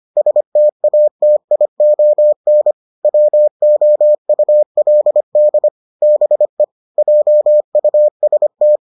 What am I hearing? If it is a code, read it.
Morse code, heard as a keyed tone, about 25 wpm, 600 Hz. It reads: STATION WOULD BE JUST